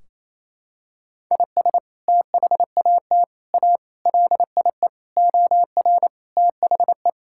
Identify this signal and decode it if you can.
Morse code, heard as a keyed tone, about 27 wpm, 700 Hz. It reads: IS THAT A LIE OR THE
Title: Sentences from Top 1600 Words 28wpm